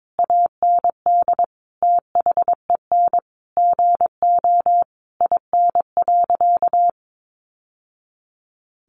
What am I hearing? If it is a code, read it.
Morse code, heard as a keyed tone, about 22 wpm, 700 Hz. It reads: AND THEN GO IN.
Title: Sentences from Top 700 Words 22wpm